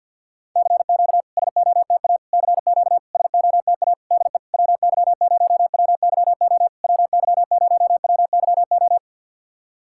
Morse code, heard as a keyed tone, about 50 wpm, 700 Hz. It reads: CQ SOTA CQ SOTA DE WY0WYO WY0WYO
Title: SOTA QSO Practice 50wpm